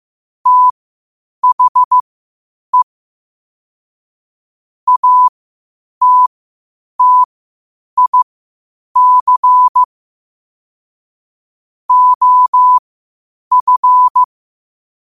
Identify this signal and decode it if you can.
Morse code, heard as a keyed tone, about 15 wpm, spaced as for 8 wpm, 1000 Hz. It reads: THE ATTIC OF
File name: qotdfiles / 8wpm